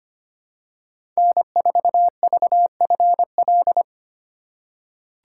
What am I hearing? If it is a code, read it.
Morse code, heard as a keyed tone, about 25 wpm, 700 Hz. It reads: N4VFL
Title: USA Call Signs - 5 Characters - 4x Character Spacing - 25wpm